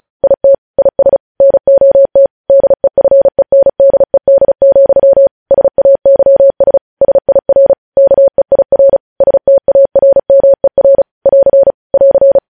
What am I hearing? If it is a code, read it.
Morse code, heard as a keyed tone, about 36 wpm, 550 Hz. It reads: IT IS NOT DEFENDED, SAYS SIR KEIR STARMER <AR> <AR>
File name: podcast / News headlines-1x-35x35